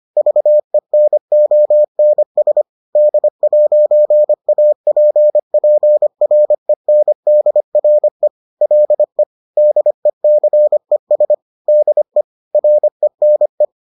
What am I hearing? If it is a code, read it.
Morse code, heard as a keyed tone, about 25 wpm, 600 Hz. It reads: VENONS D'APPRENDRE LE DECES DE RENE
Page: bulletin UFT sous forme texte et en CW